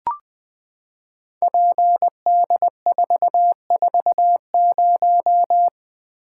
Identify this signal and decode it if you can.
Morse code, heard as a keyed tone, about 20 wpm, 700 Hz. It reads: PD440
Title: Code Groups - 5 Letter:Numbers - 20wpm